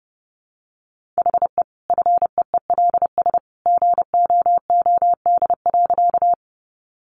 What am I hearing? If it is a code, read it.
Morse code, heard as a keyed tone, about 30 wpm, 700 Hz. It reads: HE FEELS GOOD.
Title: Additional Easy Phrases 30wpm